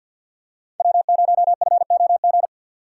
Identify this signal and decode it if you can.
Morse code, heard as a keyed tone, about 53 wpm, 700 Hz. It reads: W0FOG
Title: USA Call Signs - 5 Characters - 3x Character Spacing - 50wpm